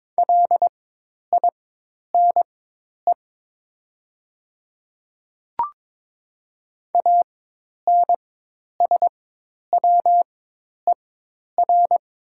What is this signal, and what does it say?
Morse code, heard as a keyed tone, about 22 wpm, spaced as for 10 wpm, 700 Hz. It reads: LINE ANSWER
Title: 4x Character Spacing - Top 300 Words - 22wpm